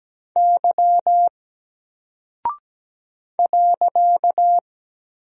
Morse code, heard as a keyed tone, about 17 wpm, 700 Hz. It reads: Y .
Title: Single Character (Letter:Number:Punctuation:Prosign) Alt 17wpm